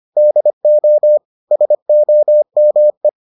Morse code, heard as a keyed tone, about 25 wpm, 600 Hz. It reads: DO SOME